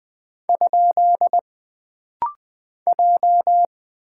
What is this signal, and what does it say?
Morse code, heard as a keyed tone, about 20 wpm, 700 Hz. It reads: ? J
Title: Single Character (Letter:Number:Punctuation:Prosign) - Rapid-Fire - 20wpm